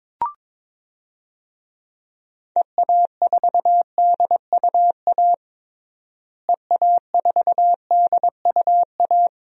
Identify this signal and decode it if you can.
Morse code, heard as a keyed tone, about 22 wpm, 700 Hz. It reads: EA4DUA EA4DUA
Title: DXCC-100 Call Signs - 6 Characters - Repeated 3x - 22wpm